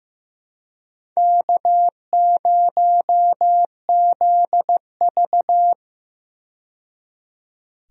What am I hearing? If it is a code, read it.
Morse code, heard as a keyed tone, about 15 wpm, 700 Hz. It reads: K0ZV
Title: USA Call Signs - 4 Characters - 15wpm